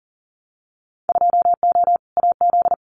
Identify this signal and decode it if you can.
Morse code, heard as a keyed tone, about 40 wpm, 700 Hz. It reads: 2O AZ
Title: Winter Field Day Exchanges 40wpm